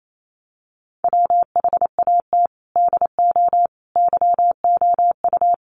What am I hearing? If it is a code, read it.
Morse code, heard as a keyed tone, about 28 wpm, 700 Hz. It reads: WHAT DO YOU
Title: Sets of 5 Words Encore2 28wpm